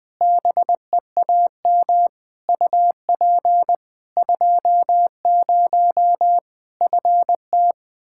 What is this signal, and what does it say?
Morse code, heard as a keyed tone, about 21 wpm, 700 Hz. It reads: BEAM UP 20 FT
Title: Practice QSO - Exchange 2 - Info - 20wpm